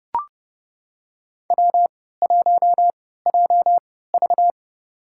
Morse code, heard as a keyed tone, about 30 wpm, spaced as for 17 wpm, 700 Hz. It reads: W1JV
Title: USA Call Signs - 4 Characters - 3x Character Spacing - 30wpm